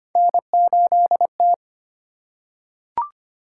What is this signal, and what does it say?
Morse code, heard as a keyed tone, about 25 wpm, 700 Hz. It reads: N8T
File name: Three Letter-Number Combinations - 25wpm